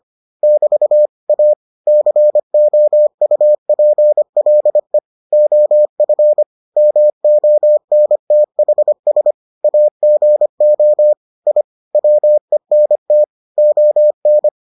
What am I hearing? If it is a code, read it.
Morse code, heard as a keyed tone, about 25 wpm, 600 Hz. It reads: <BT> A COUPLE OF MONTHS AGO I WENT ON